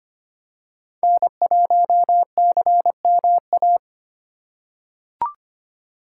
Morse code, heard as a keyed tone, about 25 wpm, 700 Hz. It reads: N1CMA
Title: USA Call Signs - 4x Character Spacing - 25wpm